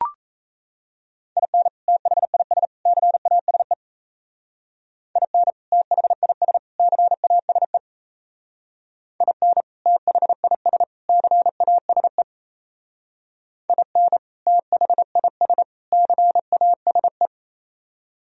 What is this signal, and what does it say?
Morse code, heard as a keyed tone, about 35 wpm, 700 Hz. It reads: IN THIS CASE IN THIS CASE IN THIS CASE IN THIS CASE